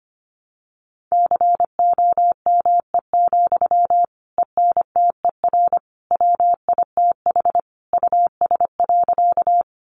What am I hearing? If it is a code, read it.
Morse code, heard as a keyed tone, about 25 wpm, 700 Hz. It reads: COME, ENTER WITH US.
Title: Sentences from Top 1300 Words 25wpm